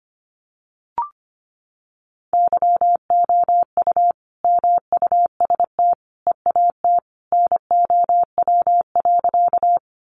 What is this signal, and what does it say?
Morse code, heard as a keyed tone, about 25 wpm, 700 Hz. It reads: YOU MUST EAT NOW.
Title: Sentences from Top 300 Words 25wpm